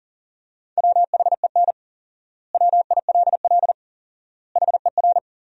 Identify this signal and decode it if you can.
Morse code, heard as a keyed tone, about 40 wpm, 700 Hz. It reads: WHEN WILL HER
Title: 4x Word-Spacing - Sentences from Top 200 Words - 40wpm